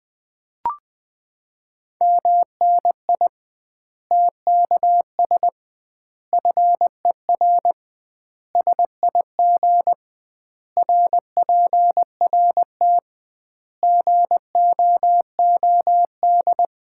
Morse code, heard as a keyed tone, about 21 wpm, 700 Hz. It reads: MNI TKS FER SIG RPRT GOOD
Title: Common QSO Phrases - 2x Word-Spacing - 20wpm